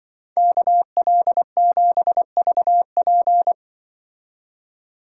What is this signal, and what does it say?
Morse code, heard as a keyed tone, about 25 wpm, 700 Hz. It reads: KL7VP
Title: Speed Racing - USA 2x2 Call Signs 20wpm